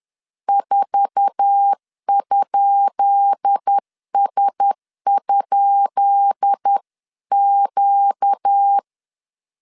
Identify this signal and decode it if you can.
Morse code, heard as a keyed tone, about 10 wpm, 800 Hz. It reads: 4?S?Q
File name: LEZIONI / 10 SIGNS / 10B INT 10wpm 2844